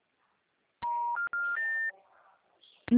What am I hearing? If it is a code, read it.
Special information tone.
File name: out- -2022- - - .2989